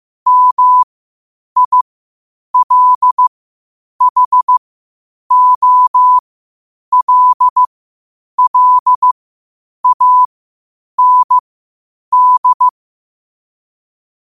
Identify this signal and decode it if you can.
Morse code, heard as a keyed tone, about 15 wpm, spaced as for 8 wpm, 1000 Hz. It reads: MILHOLLAND
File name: qotdfiles / 8wpm